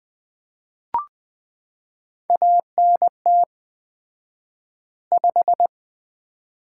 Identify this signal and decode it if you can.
Morse code, heard as a keyed tone, about 20 wpm, 700 Hz. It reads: ANT 5
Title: Common QSO Phrases - 4x Word-Spacing - 20wpm